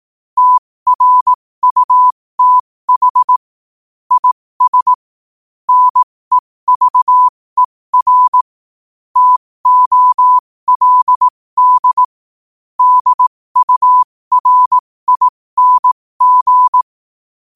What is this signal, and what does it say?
Morse code, heard as a keyed tone, about 18 wpm, spaced as for 15 wpm, 1000 Hz. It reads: TRUTH IS NEVER TOLD DURING
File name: qotdfiles / 15wpm